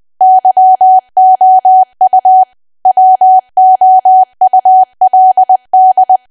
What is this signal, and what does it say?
Morse code, heard as a keyed tone, about 20 wpm, 750 Hz. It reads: YOU WOULD
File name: cwa / W1AW-Level3 / 20WPM